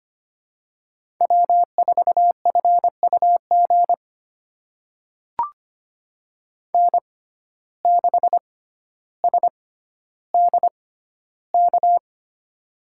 Morse code, heard as a keyed tone, about 25 wpm, spaced as for 13 wpm, 700 Hz. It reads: W4FUG N6SDK
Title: USA Call Signs - 5 Characters - 6x Character Spacing - 25wpm